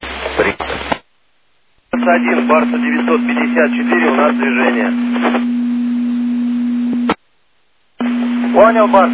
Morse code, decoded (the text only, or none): none